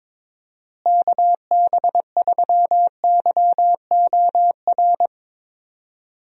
Morse code, heard as a keyed tone, about 21 wpm, 700 Hz. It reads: KB3YOR
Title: USA Call Signs - 6 Characters - Repeated 2x - 22wpm